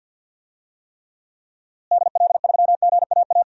Morse code, heard as a keyed tone, about 52 wpm, 700 Hz. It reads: DL3GAA
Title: DXCC-100 Call Signs - 6 Characters - Repeated 3x - 50wpm